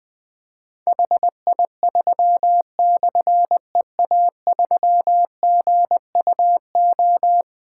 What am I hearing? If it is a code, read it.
Morse code, heard as a keyed tone, about 20 wpm, 700 Hz. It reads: HI3/EA3GUO
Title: Difficult Call Signs 20wpm